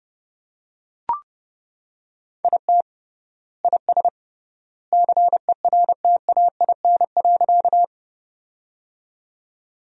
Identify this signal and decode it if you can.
Morse code, heard as a keyed tone, about 32 wpm, 700 Hz. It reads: IT IS CERTAIN.
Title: 3x Word-Spacing - Sentences from Top 500 Words - 30wpm